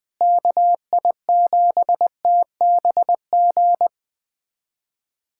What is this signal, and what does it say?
Morse code, heard as a keyed tone, about 20 wpm, 700 Hz. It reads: KI7TBG
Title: USA Callsign DE Callsign - 6 Characters - 20wpm